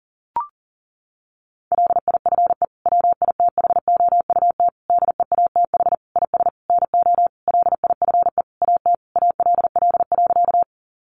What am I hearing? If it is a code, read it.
Morse code, heard as a keyed tone, about 40 wpm, 700 Hz. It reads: LIFE WITHOUT DEATH IS NO LIFE AT ALL.
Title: Sentences from Top 1100 Words 40wpm